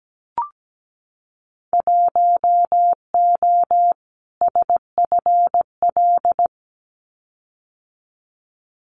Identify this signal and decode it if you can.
Morse code, heard as a keyed tone, about 17 wpm, 700 Hz. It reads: 1O SFL
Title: Winter Field Day Exchanges Encore 17wpm